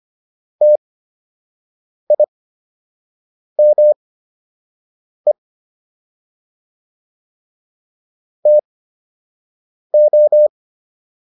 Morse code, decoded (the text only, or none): TIME TO